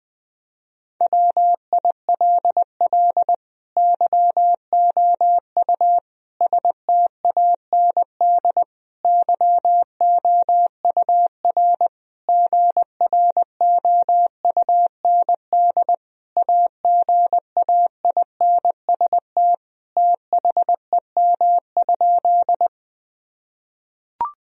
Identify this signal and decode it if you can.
Morse code, heard as a keyed tone, about 21 wpm, 700 Hz. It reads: WILL YOU STAND YOUR GROUND AGAINST THEM?
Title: Sentences from Top 1000 Words - Review Part 2 - 20wpm